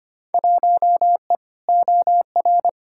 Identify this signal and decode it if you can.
Morse code, heard as a keyed tone, about 25 wpm, 700 Hz. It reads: 1E OR
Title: Field Day Exchanges Encore - 25wpm